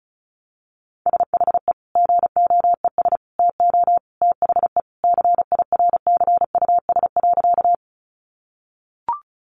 Morse code, heard as a keyed tone, about 34 wpm, 700 Hz. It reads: SHE GOES TO THE CIRCUS.
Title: Easy Phrases 35wpm